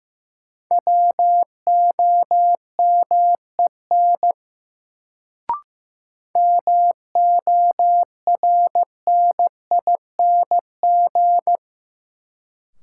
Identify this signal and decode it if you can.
Morse code, heard as a keyed tone, about 15 wpm, 700 Hz. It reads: WOMEN MORNING
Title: Sets of 3 Words - New Word Drill - 15wpm